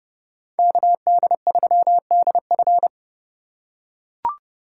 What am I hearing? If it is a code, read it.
Morse code, heard as a keyed tone, about 29 wpm, 700 Hz. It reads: KD3DF